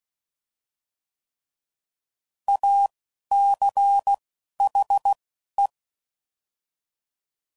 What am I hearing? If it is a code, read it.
Morse code, heard as a keyed tone, about 16 wpm, spaced as for 11 wpm, 800 Hz. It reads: ACHE
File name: Prefixes / By First Letter / As